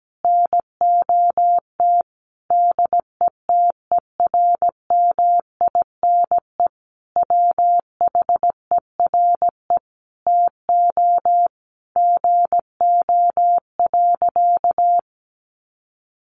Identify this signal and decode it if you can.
Morse code, heard as a keyed tone, about 17 wpm, 700 Hz. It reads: NOT DETERMINE WHERE TO GO.
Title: Sentences from Top 700 Words 17wpm